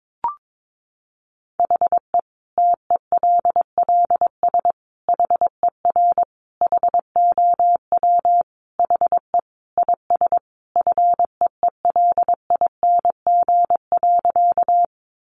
Morse code, heard as a keyed tone, about 22 wpm, 700 Hz. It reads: HE TELLS HER HOW HE IS FEELING.
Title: Additional Easy Phrases 22wpm